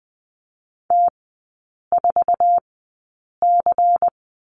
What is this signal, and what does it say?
Morse code, heard as a keyed tone, about 20 wpm, spaced as for 8 wpm, 690 Hz. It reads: T4C
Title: 191 Lesson 31e - Letter v - Three Characters 20wpm